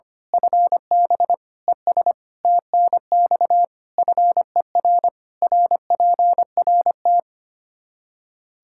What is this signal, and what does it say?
Morse code, heard as a keyed tone, about 25 wpm, 700 Hz. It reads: FB ES TNX FER RPRT
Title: Practice QSO - Exchange 2 - Info - 25wpm